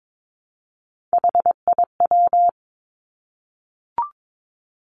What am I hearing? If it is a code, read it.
Morse code, heard as a keyed tone, about 22 wpm, 700 Hz. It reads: HIW
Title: Three Challenging Letter-Number Combinations 22wpm